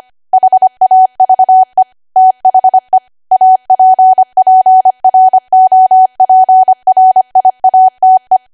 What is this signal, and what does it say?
Morse code, heard as a keyed tone, about 25 wpm, 750 Hz. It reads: HAVE THE APPROPRIATE
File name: Archive / 25 WPM / 25WPM